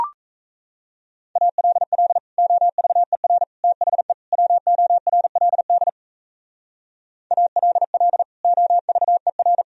Morse code, heard as a keyed tone, about 33 wpm, 700 Hz. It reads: ALL OVER THE WORLD ALL OVER
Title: Speed-Racing - Sets of 4 Words 28wpm